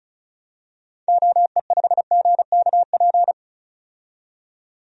Morse code, heard as a keyed tone, about 36 wpm, 700 Hz. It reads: OE5GKP
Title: DXCC-100 Call Signs - 6 Characters - 35wpm